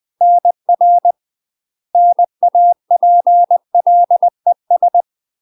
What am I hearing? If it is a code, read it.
Morse code, heard as a keyed tone, about 20 wpm, 700 Hz. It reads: NR NAPLES